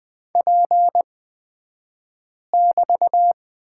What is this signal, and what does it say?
Morse code, heard as a keyed tone, about 20 wpm, spaced as for 5 wpm, 700 Hz. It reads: P<BT>
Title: 223 Lesson 36d - BT - Two Characters 20wpm